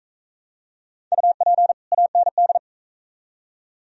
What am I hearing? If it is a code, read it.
Morse code, heard as a keyed tone, about 40 wpm, 700 Hz. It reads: UP AND